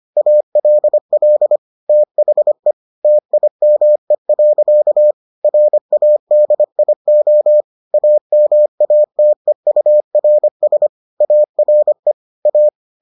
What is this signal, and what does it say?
Morse code, heard as a keyed tone, about 25 wpm, 600 Hz. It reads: ALL THE TIME. RADIO AMATEURS ARE A